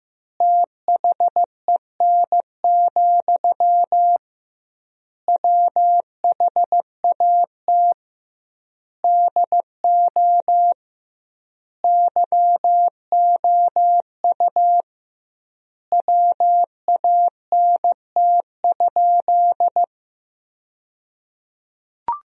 Morse code, heard as a keyed tone, about 15 wpm, 700 Hz. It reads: THEN, WHAT DO YOU WANT?
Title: 2x Word-Spacing - Sentences from Top 200 Words - 15wpm